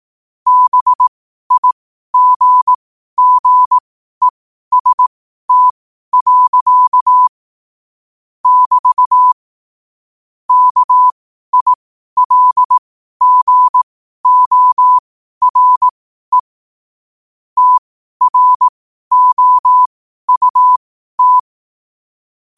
Morse code, heard as a keyed tone, about 18 wpm, spaced as for 12 wpm, 1000 Hz. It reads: BIGGEST. <BT> KILGORE TROUT
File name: qotdfiles / 12wpm